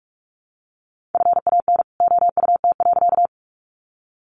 Morse code, heard as a keyed tone, about 44 wpm, 700 Hz. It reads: FAN OUT.
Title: Sentences from Top 1300 Words 45wpm